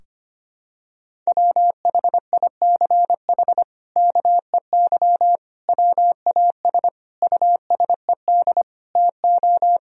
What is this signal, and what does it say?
Morse code, heard as a keyed tone, about 25 wpm, 700 Hz. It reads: WHICH KEY WAS USED TO
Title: Sentences from Top 1000 Words - Review Part 4 - 25wpm